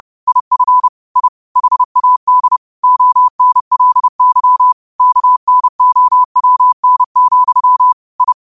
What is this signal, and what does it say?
Morse code, heard as a keyed tone, about 30 wpm, 1000 Hz. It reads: IF I HAD ONLY KNOWN, I
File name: qotdfiles / 30wpm